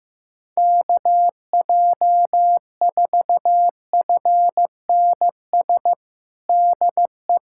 Morse code, heard as a keyed tone, about 15 wpm, 700 Hz. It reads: KJ4FNS DE
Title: USA Callsign DE Callsign 15wpm